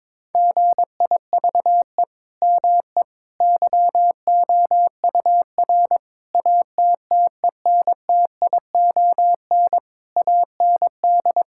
Morse code, heard as a keyed tone, about 22 wpm, 700 Hz. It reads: GIVE ME YOUR ATTENTION AND
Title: Sentences from Top 1000 Words - Review Part 6 - 22wpm